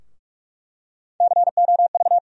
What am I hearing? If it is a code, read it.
Morse code, heard as a keyed tone, about 46 wpm, 700 Hz. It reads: COV
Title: USA Call Sign Postfixes - 3 Letters 45wpm